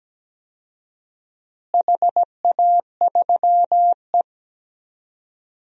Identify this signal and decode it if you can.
Morse code, heard as a keyed tone, about 17 wpm, 700 Hz. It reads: HA3E
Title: DXCC-100 Call Signs - 4 Characters - Repeated 2x - 17wpm